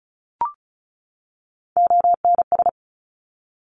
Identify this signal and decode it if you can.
Morse code, heard as a keyed tone, about 36 wpm, 700 Hz. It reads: ONS
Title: Top Three Letter Combinations in Words 35wpm